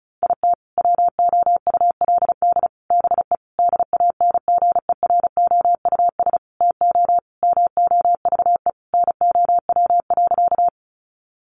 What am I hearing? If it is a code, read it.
Morse code, heard as a keyed tone, about 34 wpm, 700 Hz. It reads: IT WOULD BE DANGEROUS TO MOVE NOW.
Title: Sentences from Top 1200 Words 35wpm